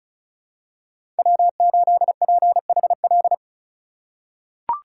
Morse code, heard as a keyed tone, about 34 wpm, 700 Hz. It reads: W8PHL
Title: USA Call Signs - 5 Characters - 2x Character Spacing - 35wpm